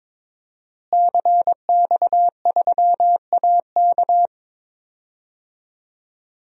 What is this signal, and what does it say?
Morse code, heard as a keyed tone, about 22 wpm, 700 Hz. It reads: CX3AK
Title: DXCC-100 Call Signs - 5 Characters - Repeated 2x - 22wpm